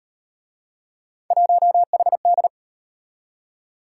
Morse code, heard as a keyed tone, about 38 wpm, 700 Hz. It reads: 1HD